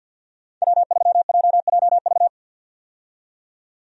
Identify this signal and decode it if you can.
Morse code, heard as a keyed tone, about 48 wpm, 700 Hz. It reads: W3JJV
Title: USA Call Signs - 5 Characters - 50wpm